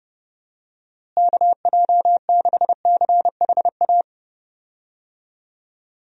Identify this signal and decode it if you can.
Morse code, heard as a keyed tone, about 32 wpm, 700 Hz. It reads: KJ6CHA
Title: USA Call Signs - 6 Characters - Repeated 2x - 30wpm